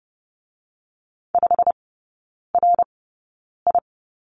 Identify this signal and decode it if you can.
Morse code, heard as a keyed tone, about 30 wpm, spaced as for 9 wpm, 700 Hz. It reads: 5RI
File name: course / 30:10wpm / 087 Lesson 15e - Number 5 - Three Characters 30wpm